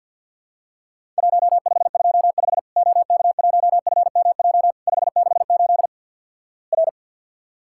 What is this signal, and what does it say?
Morse code, heard as a keyed tone, about 50 wpm, 700 Hz. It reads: 1525 OK1FMJ 568 R
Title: Worked All Europe DX Contest - QTC Exchange - 50wpm